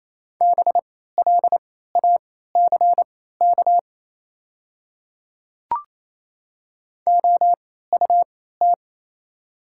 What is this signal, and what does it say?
Morse code, heard as a keyed tone, about 28 wpm, spaced as for 16 wpm, 700 Hz. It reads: BLACK OUT
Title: 3x Character Spacing - Top 500 Words - 28wpm